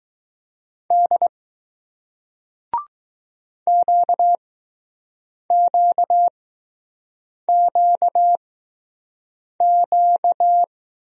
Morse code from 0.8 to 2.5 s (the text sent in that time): D